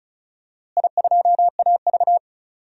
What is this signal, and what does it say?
Morse code, heard as a keyed tone, about 36 wpm, 700 Hz. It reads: I2AV
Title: DXCC-100 Call Signs - 4 Characters - Repeated 2x - 35wpm